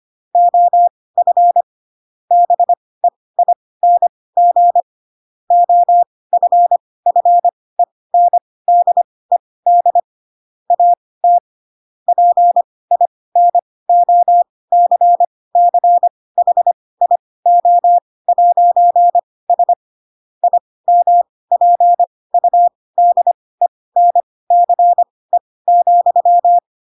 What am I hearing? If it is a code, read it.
Morse code, heard as a keyed tone, about 25 wpm, spaced as for 18 wpm, 700 Hz. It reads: OF BEING OFFENDED AT PINOCCHIO'S IMPUDENCE,
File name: Pinocchio / Pinocchio ch4 12 25 ik1hge